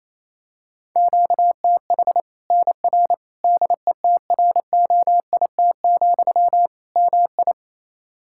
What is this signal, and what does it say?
Morse code, heard as a keyed tone, about 28 wpm, 700 Hz. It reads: QTH NR DETROIT, MI
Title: K1USN SST QSO Practice 28wpm